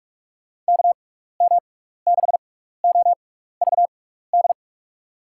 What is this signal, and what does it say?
Morse code, heard as a keyed tone, about 44 wpm, spaced as for 15 wpm, 700 Hz. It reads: KM6OVD
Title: USA Call Signs - 6 Characters - 6x Character Spacing - 45wpm